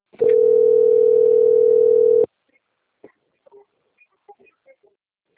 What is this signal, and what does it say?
Ringback tone (North America), 440 + 480 Hz.